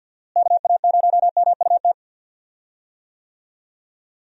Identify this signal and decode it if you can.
Morse code, heard as a keyed tone, about 50 wpm, 700 Hz. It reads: KA0MUT
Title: USA Call Signs - 6 Characters - Repeated 3x - 50wpm